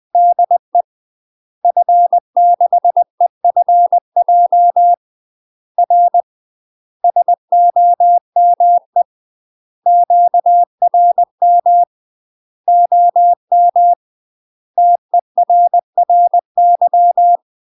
Morse code, heard as a keyed tone, about 20 wpm, 700 Hz. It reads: DE F6EFJ R SOME QRM OM TERRY
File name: UK MASTER QSOs TRAINING / 20WPM / UK MASTER QSOs 0059